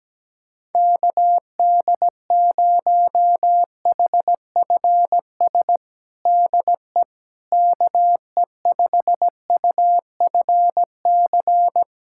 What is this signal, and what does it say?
Morse code, heard as a keyed tone, about 17 wpm, 700 Hz. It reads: KD0HFS DE KE5UFC
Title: USA Callsign DE Callsign - 6 Characters - 17wpm